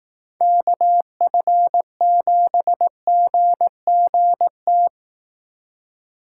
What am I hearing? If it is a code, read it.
Morse code, heard as a keyed tone, about 18 wpm, 700 Hz. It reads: KF7GGT
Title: Speed Racing - USA 2x3 Call Signs 15wpm